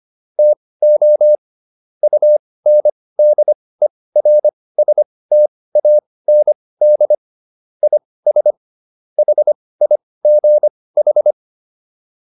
Morse code, heard as a keyed tone, about 25 wpm, spaced as for 18 wpm, 600 Hz. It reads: TO UNDERSTAND IS HIGH